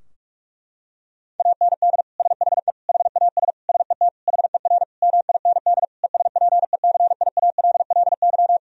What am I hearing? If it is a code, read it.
Morse code, heard as a keyed tone, about 46 wpm, 700 Hz. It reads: AND SHE HAS SET HER MIND ESPECIALLY
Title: Sentences from Top 1000 Words - Review Part 4 - 45wpm